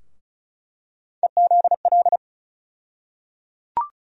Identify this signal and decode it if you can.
Morse code, heard as a keyed tone, about 36 wpm, 700 Hz. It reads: EZL
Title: USA Call Sign Postfixes - 3 Letters 35wpm